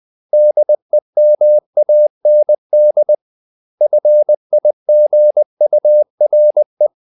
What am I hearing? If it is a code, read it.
Morse code, heard as a keyed tone, about 20 wpm, 600 Hz. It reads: DEMAND FIGURE
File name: CW / Practice / 500MostCommonEnglishWords20